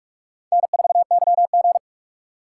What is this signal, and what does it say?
Morse code, heard as a keyed tone, about 46 wpm, 700 Hz. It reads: N4YG
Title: USA Call Signs - 3x Character Spacing - 45wpm